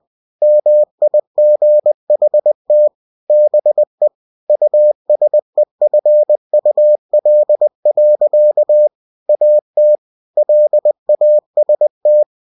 Morse code, heard as a keyed tone, about 20 wpm, 600 Hz. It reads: MIGHT BE USEFUL. AT LAST
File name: cw-books / winnie-the-pooh / winnie-20wpm-page-0109